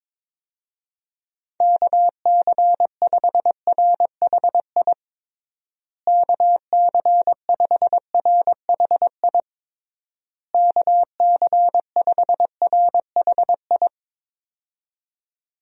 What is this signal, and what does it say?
Morse code, heard as a keyed tone, about 22 wpm, 700 Hz. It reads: KC5RHI KC5RHI KC5RHI
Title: USA Call Signs - 6 Characters - Repeated 3x - 22wpm